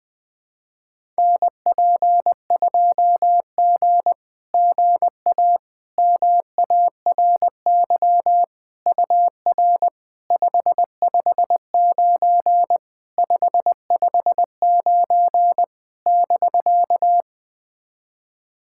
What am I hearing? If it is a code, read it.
Morse code, heard as a keyed tone, about 20 wpm, 690 Hz. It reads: NP2G GA MARY UR 559 559 <BK>
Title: SOTA QSO Practice 20wpm